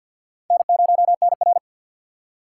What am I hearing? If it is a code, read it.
Morse code, heard as a keyed tone, about 65 wpm, spaced as for 57 wpm, 700 Hz. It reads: N0NR
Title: Speed Racing - USA Call Signs 50wpm